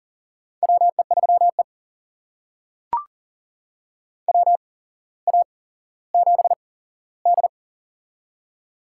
Morse code, heard as a keyed tone, about 40 wpm, spaced as for 17 wpm, 700 Hz. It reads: WE3E WA7D
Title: USA Call Signs - 4 Characters - 8x Character Spacing - 40wpm